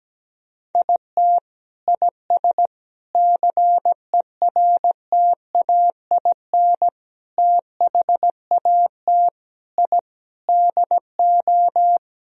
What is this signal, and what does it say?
Morse code, heard as a keyed tone, about 17 wpm, 700 Hz. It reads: IT IS CERTAIN THAT I DO
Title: Sentences from Top 1000 Words - Review Part 2 - 17wpm